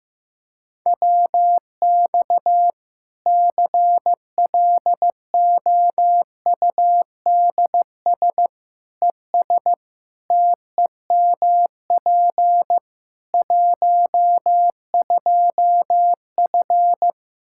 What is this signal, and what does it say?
Morse code, heard as a keyed tone, about 15 wpm, 700 Hz. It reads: WX CLOUDS ES TEMP 12F
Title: Practice QSO - Exchange 2 - Info - 15wpm